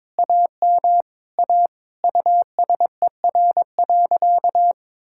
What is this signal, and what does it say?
Morse code, heard as a keyed tone, about 22 wpm, 700 Hz. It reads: AM A USER.
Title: Sentences from Top 1000 Words - Review Part 5 - 22wpm